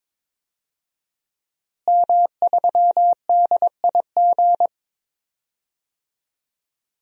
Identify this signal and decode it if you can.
Morse code, heard as a keyed tone, about 22 wpm, 700 Hz. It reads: M3DIG
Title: DXCC-100 Call Signs - 5 Characters - Repeated 3x - 22wpm